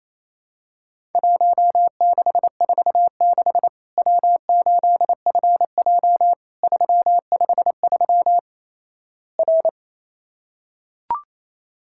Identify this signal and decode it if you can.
Morse code, heard as a keyed tone, about 27 wpm, 700 Hz. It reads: 1646 W8FJ 353 R
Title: Worked All Europe DX Contest - QTC Exchange - 28wpm